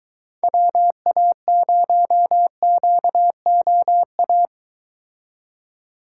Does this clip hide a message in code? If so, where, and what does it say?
Yes: Morse code from 0.4 to 4.5 s, heard as a keyed tone, about 24 wpm, 700 Hz. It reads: WA0QOA